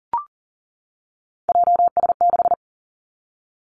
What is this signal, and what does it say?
Morse code, heard as a keyed tone, about 37 wpm, 700 Hz. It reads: JS6
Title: Three Challenging Letter-Number Combinations 40wpm